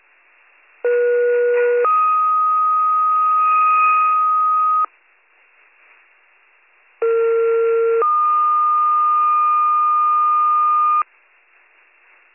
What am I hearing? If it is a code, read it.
Two-tone page, heard as a one-second tone, then a three-second tone, about 480 then 1250 Hz, about 460 then 1220 Hz. It is heard from 0.8 to 11.0 s.